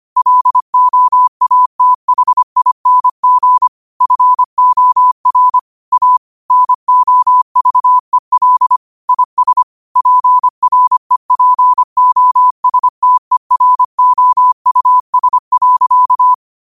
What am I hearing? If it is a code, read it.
Morse code, heard as a keyed tone, about 25 wpm, 1000 Hz. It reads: LOATHING FOR A NOVEL IS PREPOSTEROUS.